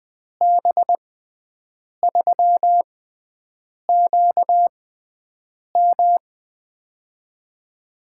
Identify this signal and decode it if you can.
Morse code, heard as a keyed tone, about 20 wpm, spaced as for 7 wpm, 700 Hz. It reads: B3QM
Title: USA Call Signs - 5 Characters - 6x Character Spacing - 20wpm